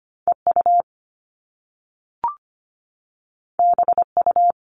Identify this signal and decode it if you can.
Morse code, heard as a keyed tone, about 25 wpm, 700 Hz. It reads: EU BU
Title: All Cut Numbers - 2-Digit 25wpm